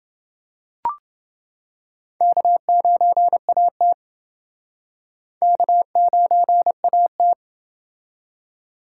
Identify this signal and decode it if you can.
Morse code, heard as a keyed tone, about 29 wpm, 700 Hz. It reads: K9AT K9AT